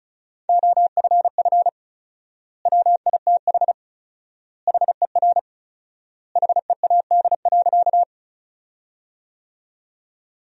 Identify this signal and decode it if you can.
Morse code, heard as a keyed tone, about 36 wpm, 700 Hz. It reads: OFF WITH HER HEAD.
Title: 4x Word-Spacing - Sentences from Top 300 Words - 35wpm